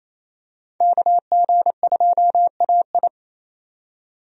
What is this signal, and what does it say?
Morse code, heard as a keyed tone, about 27 wpm, 700 Hz. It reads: KG2AI
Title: USA Call Signs - 5 Characters - 28wpm